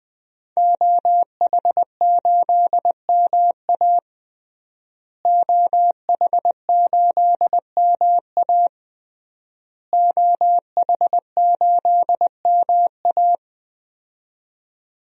Morse code, heard as a keyed tone, about 20 wpm, 700 Hz. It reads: OH8MA OH8MA OH8MA
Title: DXCC-100 Call Signs - 5 Characters - Repeated 3x - 20wpm